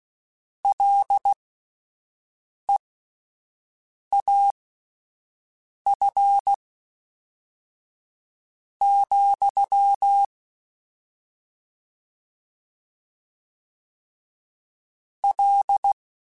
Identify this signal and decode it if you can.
Morse code, heard as a keyed tone, about 16 wpm, spaced as for 5 wpm, 800 Hz. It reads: LEAF, L